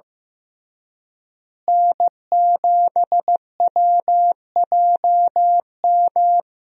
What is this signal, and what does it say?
Morse code, heard as a keyed tone, about 15 wpm, 700 Hz. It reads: N7WJM
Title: USA Call Signs - 5 Characters - Repeated 2x - 15wpm